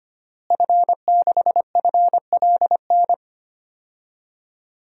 Morse code, heard as a keyed tone, about 25 wpm, 700 Hz. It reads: F6FLN
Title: DXCC-100 Call Signs - 5 Characters - Repeated 3x - 25wpm